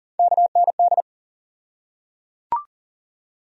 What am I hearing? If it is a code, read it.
Morse code, heard as a keyed tone, about 38 wpm, 700 Hz. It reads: KND
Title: USA Call Sign Postfixes - 3 Letters 40wpm